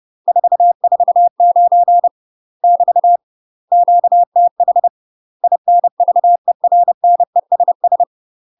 Morse code, heard as a keyed tone, about 30 wpm, 700 Hz. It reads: 449 <BT> QTH INVERNESS
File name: UK MASTER QSOs TRAINING / 30WPM / UK MASTER QSOs 0028